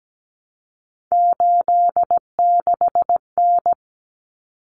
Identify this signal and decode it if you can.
Morse code, heard as a keyed tone, about 17 wpm, 700 Hz. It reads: 86N
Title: Common Cut Numbers - 3-Digit 17wpm